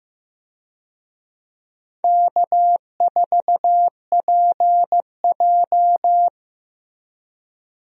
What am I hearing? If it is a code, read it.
Morse code, heard as a keyed tone, about 15 wpm, 700 Hz. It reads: K4PJ
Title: USA Call Signs - 4 Characters - Repeated 3x - 15wpm